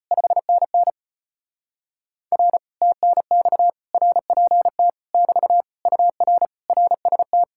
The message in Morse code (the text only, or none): 5NN R TNX RPT <BT> UR RST